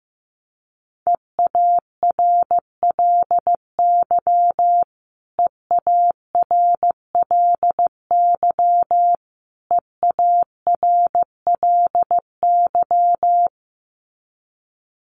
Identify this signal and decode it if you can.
Morse code, heard as a keyed tone, about 15 wpm, 700 Hz. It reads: EARLY EARLY EARLY
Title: Top 500 Words Repeated 15wpm